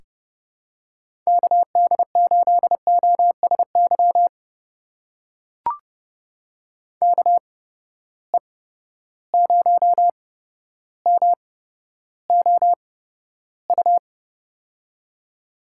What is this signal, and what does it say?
Morse code, heard as a keyed tone, about 31 wpm, spaced as for 13 wpm, 700 Hz. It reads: KD8OSY KE0MOU
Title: USA Call Signs - 6 Characters - 8x Character Spacing - 30wpm